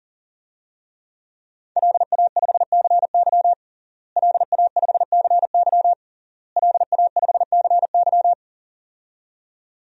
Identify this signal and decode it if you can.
Morse code, heard as a keyed tone, about 37 wpm, 700 Hz. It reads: LA5CY LA5CY LA5CY
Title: DXCC-100 Call Signs - 5 Characters - Repeated 3x - 40wpm